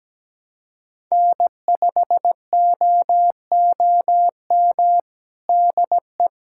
Morse code, heard as a keyed tone, about 17 wpm, 700 Hz. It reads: N5OOM DE
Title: USA Callsign DE Callsign 17wpm